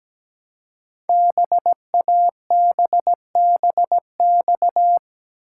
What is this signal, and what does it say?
Morse code, heard as a keyed tone, about 17 wpm, 700 Hz. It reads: BABBX
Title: Code Groups - 5 Letters - 17wpm